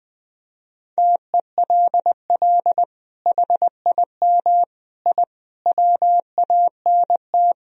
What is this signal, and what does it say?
Morse code, heard as a keyed tone, about 20 wpm, 700 Hz. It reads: TELL HIM I WANT
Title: Sentences from Top 1000 Words - Review Part 2 - 20wpm